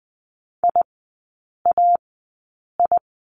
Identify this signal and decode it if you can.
Morse code, heard as a keyed tone, about 20 wpm, spaced as for 8 wpm, 700 Hz. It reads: IAI
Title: 041 Lesson 8e - Number 1 - Three Characters 20wpm